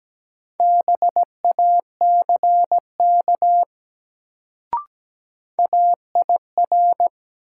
Morse code, heard as a keyed tone, about 17 wpm, 700 Hz. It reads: BACK AIR
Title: Sentences from Top 200 Words - New Word Drill - 17wpm